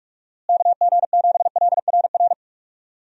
Morse code, heard as a keyed tone, about 46 wpm, 700 Hz. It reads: KG7LRR
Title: USA Call Signs - 6 Characters - Repeated 3x - 45wpm